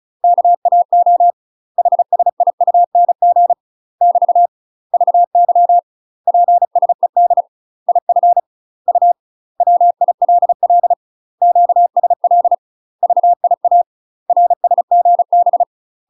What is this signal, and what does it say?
Morse code, heard as a keyed tone, about 36 wpm, 700 Hz. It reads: KAO HSIUNG <BT> VY PSED IF U WILL QSL VIA RSGB
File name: UK MASTER QSOs TRAINING / 35WPM / UK MASTER QSOs 0029